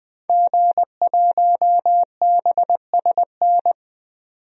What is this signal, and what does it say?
Morse code, heard as a keyed tone, about 20 wpm, 700 Hz. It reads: G1BSN
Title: DXCC-100 Call Signs - 5 Characters - 20wpm